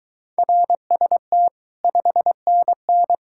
Morse code, heard as a keyed tone, about 26 wpm, 700 Hz. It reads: RST 5NN